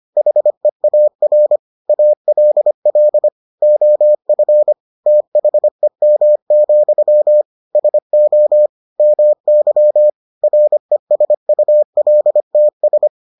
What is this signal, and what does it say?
Morse code, heard as a keyed tone, about 25 wpm, 600 Hz. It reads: HEAR ALL OF THEM, SO MY RESULTS